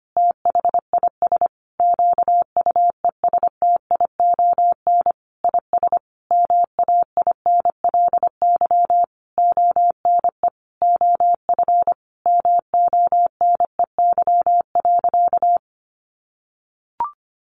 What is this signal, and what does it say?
Morse code, heard as a keyed tone, about 25 wpm, 700 Hz. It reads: THIS QUESTION IS MAINLY ONE OF MONEY.
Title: Sentences from Top 1400 Words 25wpm